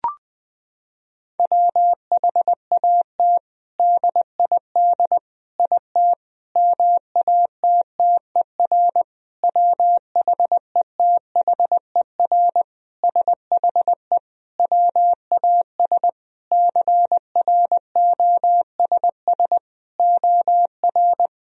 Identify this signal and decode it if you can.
Morse code, heard as a keyed tone, about 21 wpm, 700 Hz. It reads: WHAT DID IT MATTER WHETHER SHE WAS CROSS OR
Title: Sentences from Top 1000 Words - Review Part 3 - 20wpm